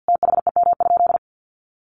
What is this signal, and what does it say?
Morse code, heard as a keyed tone, about 49 wpm, 700 Hz. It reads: THEM?
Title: Sentences from Top 1400 Words 50wpm